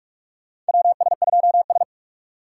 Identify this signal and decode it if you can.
Morse code, heard as a keyed tone, about 46 wpm, 700 Hz. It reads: WS2S
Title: USA Call Signs - 4 Characters - 6x Character Spacing - 45wpm